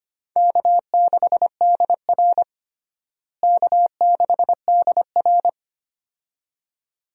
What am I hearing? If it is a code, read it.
Morse code, heard as a keyed tone, about 25 wpm, 700 Hz. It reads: K6DR K6DR
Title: USA Call Signs - 4 Characters - Repeated 2x - 25wpm